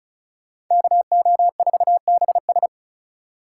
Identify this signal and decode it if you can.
Morse code, heard as a keyed tone, about 43 wpm, 700 Hz. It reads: KO4BS
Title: Speed Racing - USA 2x2 Call Signs 35wpm